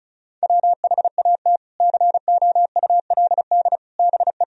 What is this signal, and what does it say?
Morse code, heard as a keyed tone, about 36 wpm, 700 Hz. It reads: WHAT COULD BE
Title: Sentences from Top 400 Words and Abbreviations 35wpm